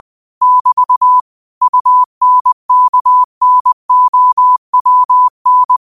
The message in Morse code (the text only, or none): <BT> UNKNOWN